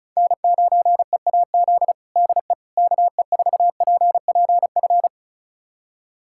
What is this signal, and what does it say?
Morse code, heard as a keyed tone, about 36 wpm, 700 Hz. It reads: N9EAZ DE KE4PPF
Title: USA Callsign DE Callsign 35wpm